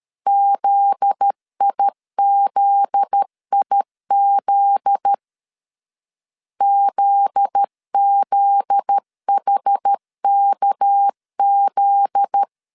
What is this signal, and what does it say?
Morse code, heard as a keyed tone, about 13 wpm, 800 Hz. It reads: ZIZIZ ZZHKZ
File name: LEZIONI / 7 GF9Z / 7 GF9Z 12wpm 2147